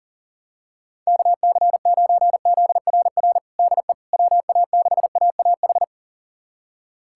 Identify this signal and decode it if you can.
Morse code, heard as a keyed tone, about 40 wpm, 700 Hz. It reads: KC9ZRR DE WA6AAH
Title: USA Callsign DE Callsign 40wpm